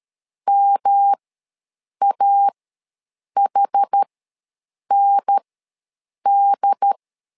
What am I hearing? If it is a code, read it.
Morse code, heard as a keyed tone, about 12 wpm, spaced as for 7 wpm, 800 Hz. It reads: MAHND
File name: LEZIONI / 3 MWRO / 3 M ws 12wpm 1745